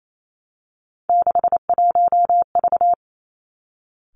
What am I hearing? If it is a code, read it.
Morse code, heard as a keyed tone, about 28 wpm, 700 Hz. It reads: 61V
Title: Three Challenging Letter-Number Combinations 28wpm